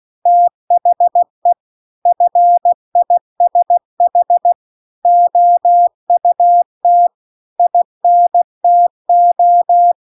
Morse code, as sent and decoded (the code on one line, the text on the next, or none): - .... / ..-. .. ... .... / --- ..- - / .. -. - ---
THE FISH OUT INTO